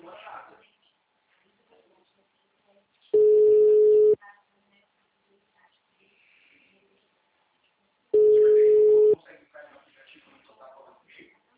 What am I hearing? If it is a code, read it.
Ringback tone (Europe, 425 Hz), with speech.
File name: out- -2010- - - .4